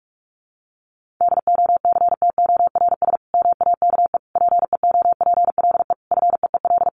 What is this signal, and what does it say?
Morse code, heard as a keyed tone, about 44 wpm, 700 Hz. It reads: DOCTORS MAKE PEOPLE FEEL
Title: Easy Phrases 45wpm